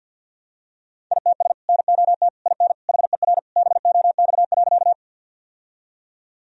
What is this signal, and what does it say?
Morse code, heard as a keyed tone, about 47 wpm, 700 Hz. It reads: ITS NOT IN HER BOX.
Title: Sentences from Top 1000 Words - Review Part 2 - 50wpm